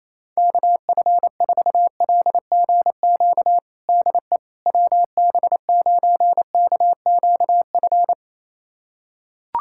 Morse code, heard as a keyed tone, about 28 wpm, 700 Hz. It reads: KF4LGQ DE WB9KQF
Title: USA Callsign DE Callsign - 6 Characters - 28wpm